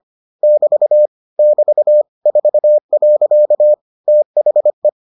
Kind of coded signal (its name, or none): Morse code